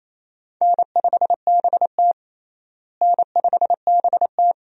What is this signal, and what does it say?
Morse code, heard as a keyed tone, about 28 wpm, 700 Hz. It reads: N5BT N5BT
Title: USA Call Signs - 4 Characters - Repeated 2x - 28wpm